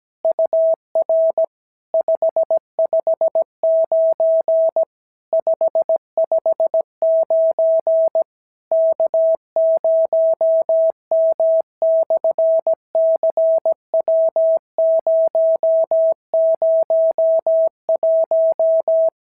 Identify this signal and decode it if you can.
Morse code, heard as a keyed tone, about 17 wpm, 650 Hz. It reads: UR 559 559 K0M/CW001
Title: SOTA QSO Practice 17wpm